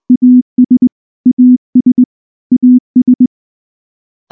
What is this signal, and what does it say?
Morse code, heard as a keyed tone, about 20 wpm, 250 Hz. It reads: AS AS AS
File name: morse / morse-training-20WPM-9